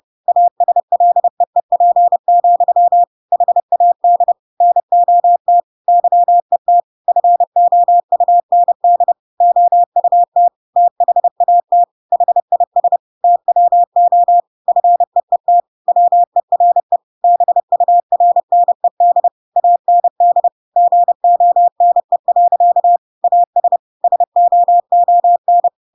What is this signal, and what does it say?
Morse code, heard as a keyed tone, about 30 wpm, 700 Hz. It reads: ASLEEP, HAD NOT YET FOUND OUT THAT HIS TWO FEET WERE BURNED AND GONE. AS SOON
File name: Pinocchio / Pinocchio ch7 30 30 ik1hge